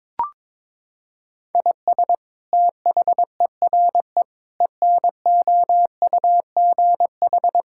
Morse code, heard as a keyed tone, about 22 wpm, 700 Hz. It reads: IS THERE ENOUGH
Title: Sentences from Top 1000 Words - Review Part 5 - 22wpm